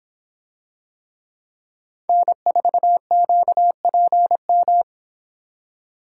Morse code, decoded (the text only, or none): N4QPM